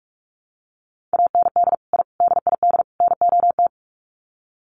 Morse code, heard as a keyed tone, about 44 wpm, 700 Hz. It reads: AND I DID NOT
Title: Sets of 4 Words 45wpm